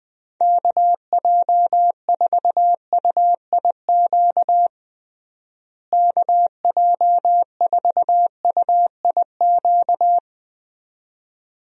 Morse code, heard as a keyed tone, about 21 wpm, 700 Hz. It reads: KJ4UIQ KJ4UIQ
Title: USA Call Signs - 6 Characters - Repeated 2x - 20wpm